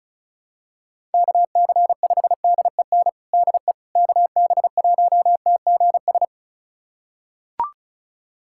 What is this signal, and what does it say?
Morse code, heard as a keyed tone, about 34 wpm, 700 Hz. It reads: KC5DEN DE KB1TGS
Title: USA Callsign DE Callsign - 6 Characters - 35wpm